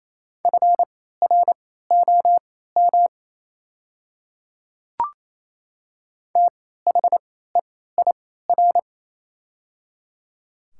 Morse code, heard as a keyed tone, about 28 wpm, spaced as for 16 wpm, 700 Hz. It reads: FROM THEIR
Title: 3x Character Spacing - Top 100 Words - 28wpm